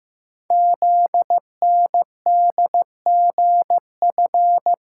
Morse code, heard as a keyed tone, about 15 wpm, 700 Hz. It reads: ZNDGF
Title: Code Groups - 5 Letters - 15wpm